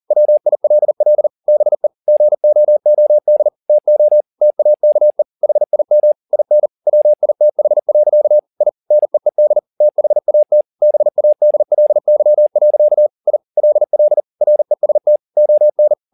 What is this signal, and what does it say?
Morse code, heard as a keyed tone, about 40 wpm, 600 Hz. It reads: WILL BE GOOD TO TAKE HIM IN WITH. I NEED THAT BADLY. I LL REST ON
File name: the old man and the sea 2 40wpm